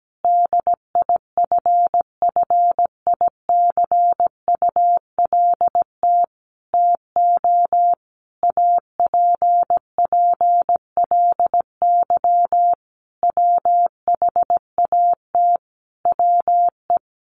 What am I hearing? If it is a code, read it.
Morse code, heard as a keyed tone, about 17 wpm, 700 Hz. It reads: DIFFICULT TO APPLY WHAT WE
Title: Sentences from Top 1200 Words 17wpm